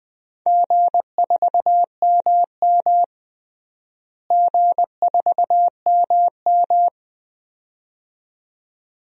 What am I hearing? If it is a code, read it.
Morse code, heard as a keyed tone, about 20 wpm, 700 Hz. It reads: G4MM G4MM
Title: DXCC-100 Call Signs - 4 Characters - Repeated 2x - 20wpm